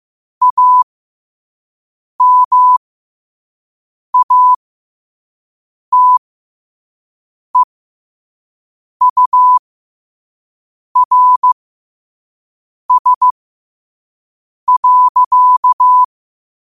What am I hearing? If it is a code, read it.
Morse code, heard as a keyed tone, about 15 wpm, spaced as for 5 wpm, 1000 Hz. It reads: AMATEURS.
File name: qotdfiles / 5wpm